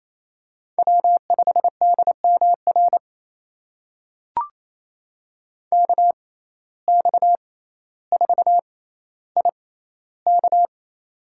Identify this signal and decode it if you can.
Morse code, heard as a keyed tone, about 28 wpm, spaced as for 14 wpm, 700 Hz. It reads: W5DMR KX4IK
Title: USA Call Signs - 5 Characters - 6x Character Spacing - 28wpm